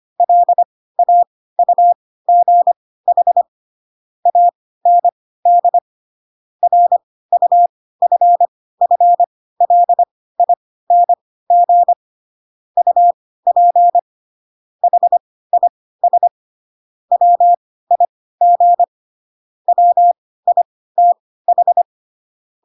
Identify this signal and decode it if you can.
Morse code, heard as a keyed tone, about 25 wpm, spaced as for 16 wpm, 700 Hz. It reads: LAUGH AND RUFFLING UP HIS WIG WITH
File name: Pinocchio / Pinocchio ch1 10 25 ik1hge